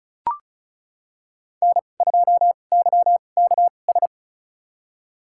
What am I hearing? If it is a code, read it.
Morse code, heard as a keyed tone, about 34 wpm, spaced as for 25 wpm, 700 Hz. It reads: N2YKS
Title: USA Call Signs - 5 Characters - 2x Character Spacing - 35wpm